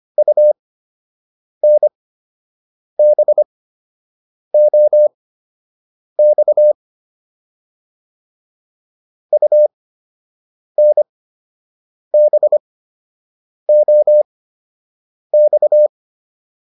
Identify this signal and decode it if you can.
Morse code, heard as a keyed tone, about 25 wpm, spaced as for 7 wpm, 600 Hz. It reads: UNBOX UNBOX